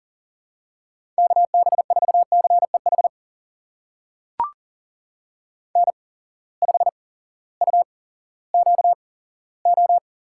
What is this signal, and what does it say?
Morse code, heard as a keyed tone, about 40 wpm, spaced as for 18 wpm, 700 Hz. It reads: KB4CEH N5UQO
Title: USA Call Signs - 8x Character Spacing - 40wpm